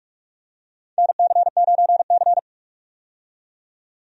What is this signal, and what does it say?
Morse code, heard as a keyed tone, about 45 wpm, 700 Hz. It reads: NC9C
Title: USA Call Signs - 4 Characters - 45wpm